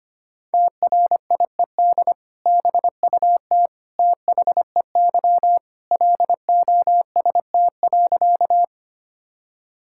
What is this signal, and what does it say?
Morse code, heard as a keyed tone, about 25 wpm, 700 Hz. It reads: TRIED BUT THEY LOST.
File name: practice / 25wpm / Sentences with 2-Word Phrases - 25wpm